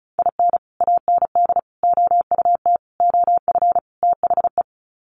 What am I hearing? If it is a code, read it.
Morse code, heard as a keyed tone, about 34 wpm, 700 Hz. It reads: IN AND OUT OF THE
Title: Sets of 5 Words 35wpm